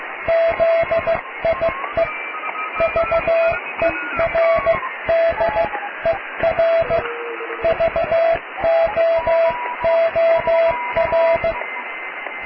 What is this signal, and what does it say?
Morse code, heard as a keyed tone, about 16 wpm, spaced as for 14 wpm, 660 Hz, with radio noise. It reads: ZIE VERDER VOOR
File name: cw / tekst(a)+QRM